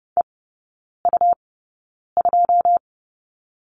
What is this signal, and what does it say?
Morse code, heard as a keyed tone, about 30 wpm, spaced as for 9 wpm, 700 Hz. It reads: EU2
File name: Lesson 17e - Letter u - Three Characters 30wpm